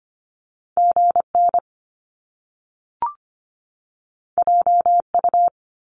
Morse code, heard as a keyed tone, about 25 wpm, 700 Hz. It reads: GN JU
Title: Top Two Letter Combinations in Words 25wpm